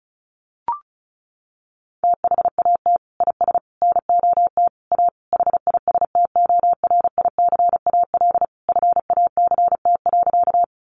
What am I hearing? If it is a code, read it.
Morse code, heard as a keyed tone, about 34 wpm, 700 Hz. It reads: THAT IS NOT A HISTORICAL FACT.
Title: Sentences from Top 600 Words 35wpm